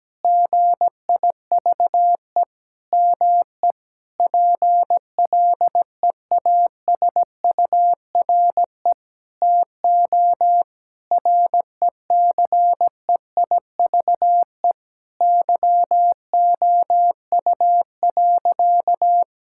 Morse code, heard as a keyed tone, about 17 wpm, 700 Hz. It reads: GIVE ME PLEASURE TO RECEIVE YOU.
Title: Sentences from Top 1600 Words 17wpm